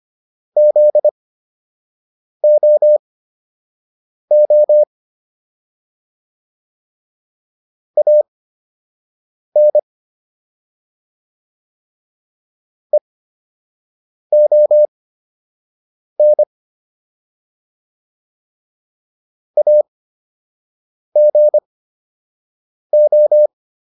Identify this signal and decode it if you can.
Morse code, heard as a keyed tone, about 25 wpm, spaced as for 6 wpm, 600 Hz. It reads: ZOO AN EON AGO